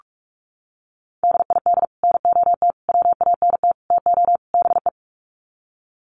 Morse code, heard as a keyed tone, about 44 wpm, 700 Hz. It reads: DID NOT WANT TO BE
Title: Sets of 5 Words Encore 45wpm